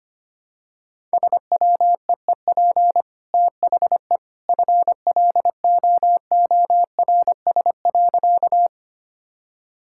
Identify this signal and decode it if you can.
Morse code, heard as a keyed tone, about 25 wpm, 700 Hz. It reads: SWEEP THE FLOORS.
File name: Easy Sentences with Hints - 25wpm